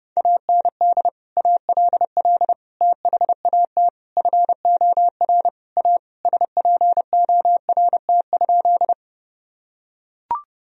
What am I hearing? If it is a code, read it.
Morse code, heard as a keyed tone, about 30 wpm, 700 Hz. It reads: AND ALL THAT FOR A SPORT?
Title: Sentences from Top 1000 Words - Review Part 4 - 30wpm